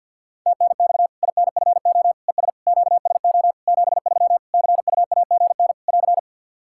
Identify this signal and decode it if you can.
Morse code, heard as a keyed tone, about 47 wpm, 700 Hz. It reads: TNX INFO ES QSO 73 CUAGN <AR>
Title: Practice QSO - Exchange 4 - Ending - 50wpm